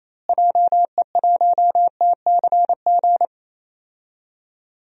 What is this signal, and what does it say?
Morse code, heard as a keyed tone, about 28 wpm, 700 Hz. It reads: JE1TCG
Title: DXCC-100 Call Signs - 6 Characters - Repeated 3x - 28wpm